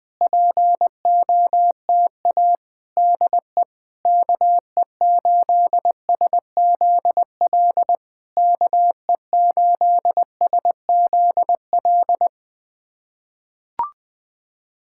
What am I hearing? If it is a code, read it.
Morse code, heard as a keyed tone, about 20 wpm, 690 Hz. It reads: POTA DE KE8SZL KE8SZL
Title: POTA QSO Practice - 20wpm